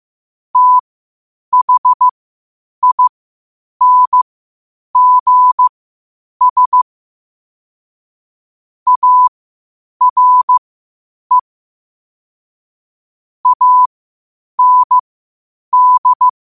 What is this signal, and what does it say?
Morse code, heard as a keyed tone, about 15 wpm, spaced as for 8 wpm, 1000 Hz. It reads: THINGS ARE AND
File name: qotdfiles / 8wpm